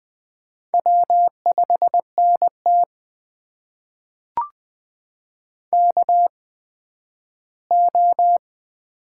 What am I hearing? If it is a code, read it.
Morse code, heard as a keyed tone, about 20 wpm, spaced as for 12 wpm, 700 Hz. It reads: W5NT KO
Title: USA Call Signs - 4 Characters - 8x Character Spacing - 20wpm